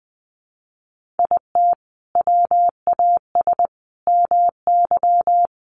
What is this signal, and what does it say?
Morse code, heard as a keyed tone, about 20 wpm, 700 Hz. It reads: IT WAS MY
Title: Sentences from Top 1500 Words 20wpm